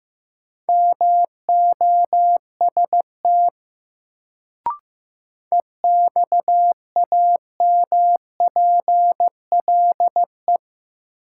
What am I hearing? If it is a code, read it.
Morse code, heard as a keyed tone, about 15 wpm, 700 Hz. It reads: MOST EXAMPLE
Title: Sets of 2 Words - New Word Drill - 15wpm